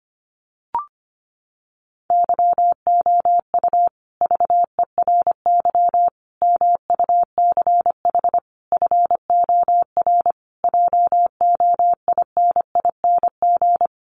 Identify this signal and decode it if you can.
Morse code, heard as a keyed tone, about 25 wpm, 700 Hz. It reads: YOU VERY MUCH FOR JOINING
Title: Sets of 5 Words Encore 25wpm